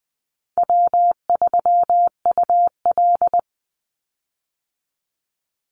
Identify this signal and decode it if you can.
Morse code, heard as a keyed tone, about 20 wpm, 700 Hz. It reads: W3UL
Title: Easy Call Signs Repeated 2x - 20wpm